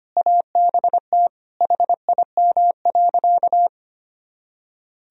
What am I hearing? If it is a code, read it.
Morse code, heard as a keyed tone, about 25 wpm, 700 Hz. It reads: ABT HIM.
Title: Sentences from Top 200 Words and Abbreviations 25wpm